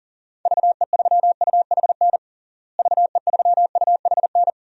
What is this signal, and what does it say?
Morse code, heard as a keyed tone, about 40 wpm, 700 Hz. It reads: VE3UHN VE3UHN
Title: DXCC-100 Call Signs - 6 Characters - Repeated 2x - 40wpm